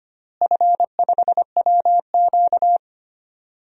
Morse code, heard as a keyed tone, about 25 wpm, 700 Hz. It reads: F5WQ
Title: DXCC-100 Call Signs - 4 Characters - Repeated 2x - 25wpm